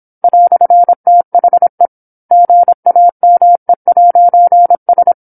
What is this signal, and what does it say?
Morse code, heard as a keyed tone, about 27 wpm, 700 Hz. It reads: "THE GAME'S
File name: 08-Chapter-08 26 WPM